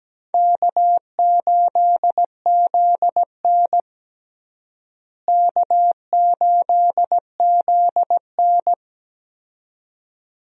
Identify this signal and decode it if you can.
Morse code, heard as a keyed tone, about 17 wpm, 700 Hz. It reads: K8ZN K8ZN
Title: USA Call Signs - 4 Characters - Repeated 2x - 17wpm